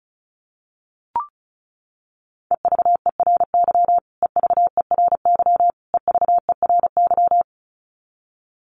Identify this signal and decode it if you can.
Morse code, heard as a keyed tone, about 34 wpm, 700 Hz. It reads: EVERY EVERY EVERY
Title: Top 500 Words Repeated 35wpm